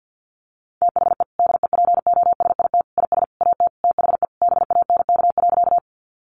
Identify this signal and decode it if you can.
Morse code, heard as a keyed tone, about 50 wpm, 700 Hz. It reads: THE DEPOSIT IS AT THE BANK.
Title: Sentences from Top 1300 Words 50wpm